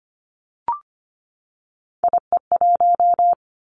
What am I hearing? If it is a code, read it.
Morse code, heard as a keyed tone, about 25 wpm, 700 Hz. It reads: IE1
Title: Three Challenging Letter-Number Combinations 25wpm